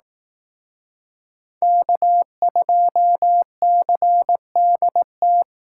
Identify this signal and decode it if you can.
Morse code, heard as a keyed tone, about 18 wpm, 700 Hz. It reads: K2CDT
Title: Speed Racing - USA 1x3 Call Signs 15wpm